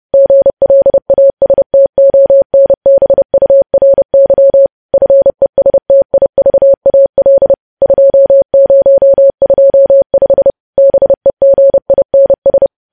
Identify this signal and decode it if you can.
Morse code, heard as a keyed tone, about 30 wpm, 550 Hz. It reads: GLASTONBURY FESTIVAL 2025 BEGINS
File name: podcast / News headlines-1x-30x30